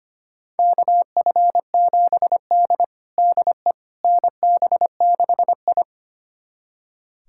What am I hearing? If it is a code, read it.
Morse code, heard as a keyed tone, about 25 wpm, 700 Hz. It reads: KF7D DE NB6I
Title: USA Callsign DE Callsign - 4 Characters - 25wpm